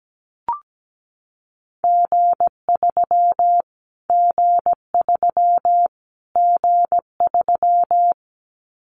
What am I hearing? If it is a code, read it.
Morse code, heard as a keyed tone, about 17 wpm, 700 Hz. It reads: G3 G3 G3
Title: Callsign Prefixes for Top 25 DXCC - Repeated 3x - 17wpm